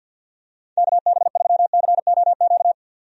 Morse code, heard as a keyed tone, about 51 wpm, 700 Hz. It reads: KB3COQ
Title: USA Call Signs - 6 Characters - 50wpm